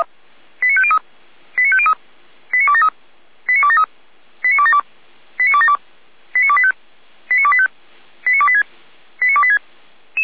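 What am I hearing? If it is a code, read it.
Five-tone selective call (ZVEI), heard as quick runs of five tones, tones of about 70 ms. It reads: 89503 89503 89273 89273 89272 89272 89285 89285 89286 89286
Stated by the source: ZVEI five-tone call received on SDR.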